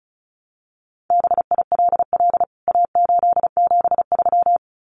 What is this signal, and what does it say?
Morse code, heard as a keyed tone, about 34 wpm, 700 Hz. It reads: BILL A873
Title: CWT Exchange 35wpm